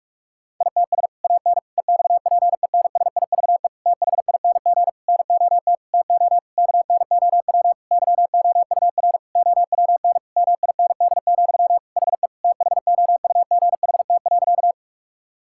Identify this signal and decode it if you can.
Morse code, heard as a keyed tone, about 46 wpm, 700 Hz. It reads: ITS AN EXPENSIVE THING NOT TO KNOW YOUR OWN MIND, HE THOUGHT.
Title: Sentences from Top 1000 Words - Review Part 6 - 45wpm